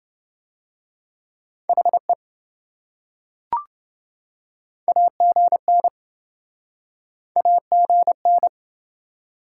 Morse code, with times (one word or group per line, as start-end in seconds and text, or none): 1.7-2.1 HE
4.9-5.9 AGN
7.4-8.5 AGN